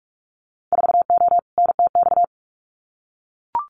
Morse code, heard as a keyed tone, about 44 wpm, 700 Hz. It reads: 4O NTX
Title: Winter Field Day Exchanges 45wpm